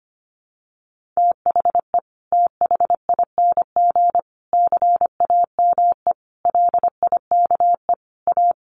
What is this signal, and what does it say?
Morse code, heard as a keyed tone, about 25 wpm, 700 Hz. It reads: THE THING CAME LIKE A
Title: Sentences from Top 1400 Words 25wpm